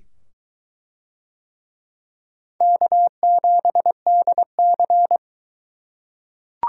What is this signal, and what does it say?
Morse code, heard as a keyed tone, about 21 wpm, 700 Hz. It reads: K7DC